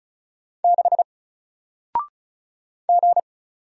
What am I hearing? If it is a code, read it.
Morse code, heard as a keyed tone, about 35 wpm, 700 Hz. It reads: 6 G
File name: Single-Letter-Number-35wpm